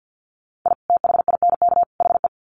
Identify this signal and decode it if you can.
Morse code, heard as a keyed tone, about 48 wpm, 700 Hz. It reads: I THINK HE
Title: Sets of 3 Words 50wpm